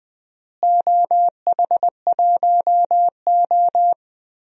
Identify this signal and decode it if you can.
Morse code, heard as a keyed tone, about 20 wpm, 700 Hz. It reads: OH1O
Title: DXCC-100 Call Signs - 4 Characters - Repeated 3x - 20wpm